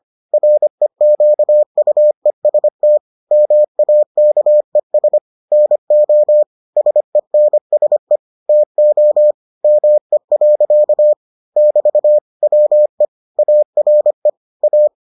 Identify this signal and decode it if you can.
Morse code, heard as a keyed tone, about 25 wpm, 600 Hz. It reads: REQUEST MAKES NO SENSE TO ME. <BT> WE ARE A